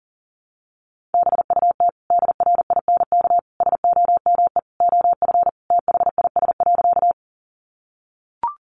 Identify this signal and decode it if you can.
Morse code, heard as a keyed tone, about 40 wpm, 700 Hz. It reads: BUT DRINK SOME OF THIS.
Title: Sentences from Top 1500 Words 40wpm